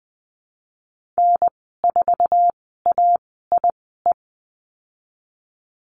Morse code, heard as a keyed tone, about 20 wpm, spaced as for 15 wpm, 700 Hz. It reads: N4AIE
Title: 055 Lesson 10g - Letter r - Call Signs 20wpm